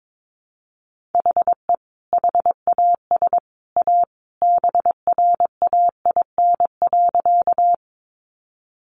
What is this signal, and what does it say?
Morse code, heard as a keyed tone, about 22 wpm, 700 Hz. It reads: HE HAS A BRAIN.
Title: Sentences from Top 1100 Words 22wpm